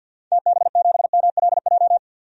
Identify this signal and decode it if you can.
Morse code, heard as a keyed tone, about 51 wpm, 700 Hz. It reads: TB7MLJ
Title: DXCC-100 Call Signs - 6 Characters - 50wpm